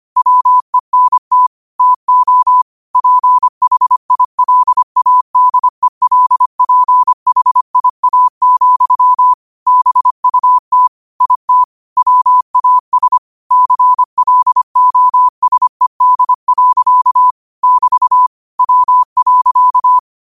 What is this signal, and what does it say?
Morse code, heard as a keyed tone, about 25 wpm, 1000 Hz. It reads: WENT TO PHILADELPHIA, BUT IT WAS CLOSED. <BT> W.
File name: qotdfiles / 25wpm